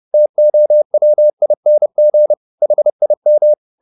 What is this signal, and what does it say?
Morse code, heard as a keyed tone, about 30 wpm, 600 Hz. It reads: TOWING HIM